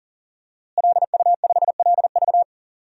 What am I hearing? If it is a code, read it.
Morse code, heard as a keyed tone, about 37 wpm, 700 Hz. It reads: LU5LV
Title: DXCC-100 Call Signs - 5 Characters - Repeated 3x - 40wpm